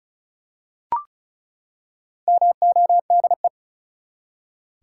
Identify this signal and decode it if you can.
Morse code, heard as a keyed tone, about 36 wpm, 700 Hz. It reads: MODE
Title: Four Letter Words - 35wpm